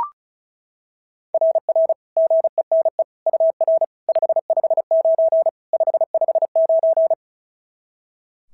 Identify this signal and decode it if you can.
Morse code, heard as a keyed tone, about 36 wpm, 690 Hz. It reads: RR GENE UR 559 559
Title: SOTA QSO Practice 35wpm